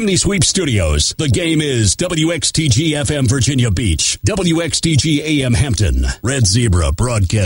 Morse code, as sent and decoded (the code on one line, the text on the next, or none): none